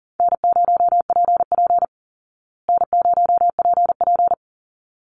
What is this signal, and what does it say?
Morse code, heard as a keyed tone, about 40 wpm, 700 Hz. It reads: N0PP N0PP
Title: Easy Call Signs Repeated 2x - 40wpm